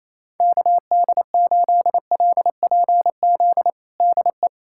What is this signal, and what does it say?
Morse code, heard as a keyed tone, about 28 wpm, 700 Hz. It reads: KD8LPZ DE
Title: USA Callsign DE Callsign - 6 Characters - 28wpm